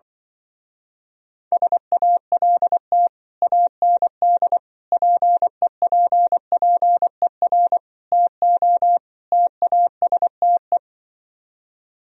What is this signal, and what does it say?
Morse code, heard as a keyed tone, about 24 wpm, 700 Hz. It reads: SALT AND PEPPER TO TASTE
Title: Speed-Racing - Sets of 5 Words 20wpm